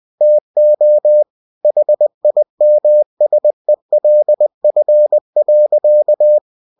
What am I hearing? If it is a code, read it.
Morse code, heard as a keyed tone, about 20 wpm, 600 Hz. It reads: TO HIMSELF.
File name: cw-books / winnie-the-pooh / winnie-20wpm-page-0020